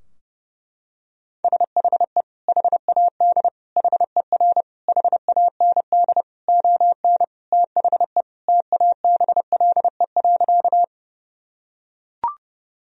Morse code, heard as a keyed tone, about 30 wpm, 700 Hz. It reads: SHE HAD HER HAND ON THE TABLE.
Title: Sentences from Top 1000 Words - Review Part 2 - 30wpm